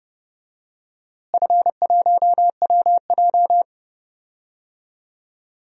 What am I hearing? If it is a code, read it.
Morse code, heard as a keyed tone, about 29 wpm, 700 Hz. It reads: F1WJ
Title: DXCC-100 Call Signs - 4 Characters - Repeated 3x - 30wpm